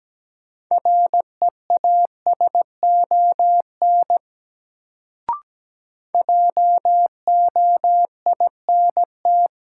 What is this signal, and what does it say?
Morse code, heard as a keyed tone, about 17 wpm, 700 Hz. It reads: REASON JOINT
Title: Sets of 5 Words Encore - New Word Drill - 17wpm